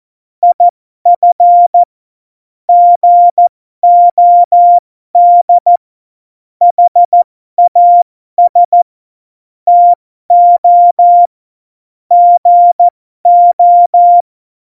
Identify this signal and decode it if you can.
Morse code, heard as a keyed tone, about 14 wpm, spaced as for 12 wpm, 700 Hz. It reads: IF GOD HAS TO GO
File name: Morsekurs / Witze